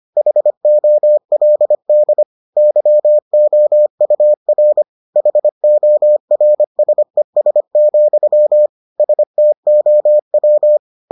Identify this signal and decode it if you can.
Morse code, heard as a keyed tone, about 25 wpm, 600 Hz. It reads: HOLD YOUR HORSES, STOW